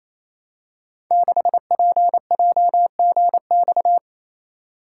Morse code, heard as a keyed tone, about 29 wpm, 700 Hz. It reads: 6PJGX
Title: Code Groups - 5 Letter:Numbers - 28wpm